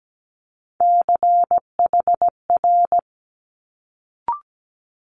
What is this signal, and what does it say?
Morse code, heard as a keyed tone, about 17 wpm, 700 Hz. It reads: CHR
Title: Multi-letter Phonemes Quick Fire 17wpm